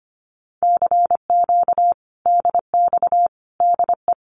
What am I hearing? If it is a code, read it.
Morse code, heard as a keyed tone, about 25 wpm, 700 Hz. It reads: CQ DX DE
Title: Common QSO Phrases Encore 25wpm